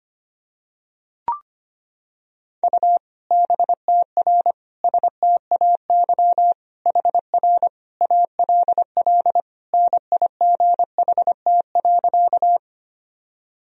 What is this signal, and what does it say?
Morse code, heard as a keyed tone, about 25 wpm, 700 Hz. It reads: U BTR STAY HR ALL NIGHT.
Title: Sentences from Top 500 Words and Abbreviations 25wpm